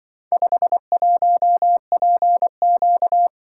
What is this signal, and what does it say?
Morse code, heard as a keyed tone, about 24 wpm, 700 Hz. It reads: 51PQ
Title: Speed-Racing - Four Letter-Number Combinations - 20wpm